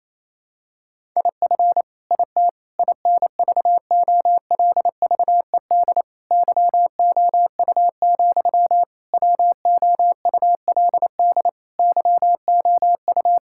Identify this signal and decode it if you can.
Morse code, heard as a keyed tone, about 28 wpm, 700 Hz. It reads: IF IT INVOLVED YOU, WOULD YOU
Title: Sentences from Top 1000 Words - Review Part 5 - 28wpm